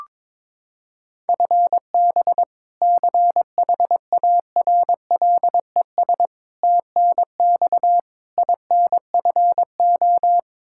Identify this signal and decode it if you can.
Morse code, heard as a keyed tone, about 22 wpm, 700 Hz. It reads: FB CHARLES TNX INFO
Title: Practice QSO - Exchange 3 - More Info - 22wpm